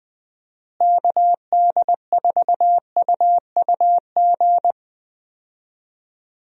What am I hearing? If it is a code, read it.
Morse code, heard as a keyed tone, about 20 wpm, 700 Hz. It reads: KD4UUG
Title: USA Call Signs - 6 Characters - Repeated 2x - 20wpm